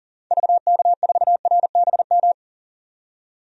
Morse code, heard as a keyed tone, about 40 wpm, 700 Hz. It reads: VK4RBM
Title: DXCC-100 Call Signs - 6 Characters - 40wpm